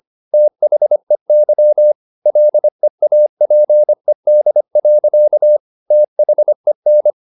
Morse code, heard as a keyed tone, about 25 wpm, 600 Hz. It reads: THEY LEAPED. THEN